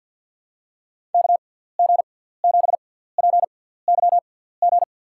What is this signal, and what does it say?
Morse code, heard as a keyed tone, about 49 wpm, spaced as for 17 wpm, 700 Hz. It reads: KG7PYG
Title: USA Call Signs - 6x Character Spacing - 50wpm